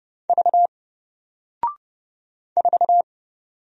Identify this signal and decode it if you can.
Morse code, heard as a keyed tone, about 30 wpm, 700 Hz. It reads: V 4
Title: Single Letter-Number Alt 30wpm